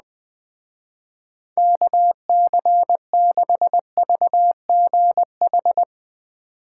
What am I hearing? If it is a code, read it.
Morse code, heard as a keyed tone, about 20 wpm, 700 Hz. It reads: KC6VGH